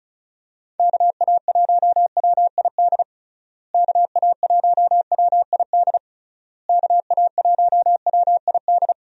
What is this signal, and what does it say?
Morse code, heard as a keyed tone, about 36 wpm, 700 Hz. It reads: KA1WID KA1WID KA1WID
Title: USA Call Signs - 6 Characters - Repeated 3x - 35wpm